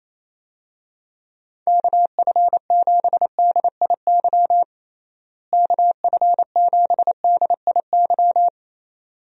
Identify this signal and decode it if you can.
Morse code, heard as a keyed tone, about 28 wpm, 700 Hz. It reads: KF7DIY KF7DIY
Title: USA Call Signs - 6 Characters - Repeated 2x - 28wpm